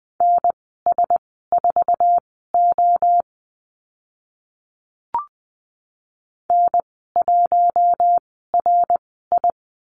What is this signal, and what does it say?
Morse code, heard as a keyed tone, about 20 wpm, spaced as for 14 wpm, 700 Hz. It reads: NS4O N1RI
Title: 055 Lesson 10g - Letter r - Call Signs 20wpm